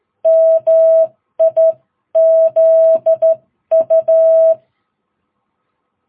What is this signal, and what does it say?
Morse code, heard as a keyed tone, about 13 wpm, spaced as for 11 wpm, 660 Hz. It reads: MIZU
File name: jh8cbh / 2022cw